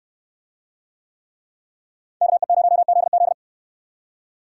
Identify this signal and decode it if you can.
Morse code, heard as a keyed tone, about 64 wpm, 700 Hz. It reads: KE0ZZ